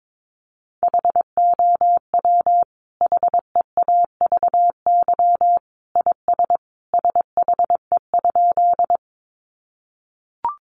Morse code, heard as a keyed tone, about 22 wpm, 700 Hz. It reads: HOW HEAVY IS SHE?
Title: Sentences from Top 1100 Words 22wpm